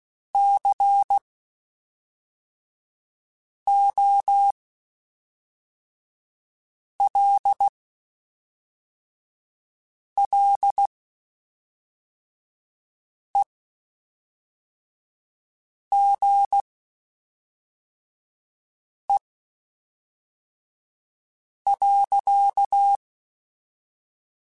Morse code, heard as a keyed tone, about 16 wpm, spaced as for 3 wpm, 800 Hz. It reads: COLLEGE.
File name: (spaced) / QSOs Spaced-mp3s / WD0BGP 1 @5 sp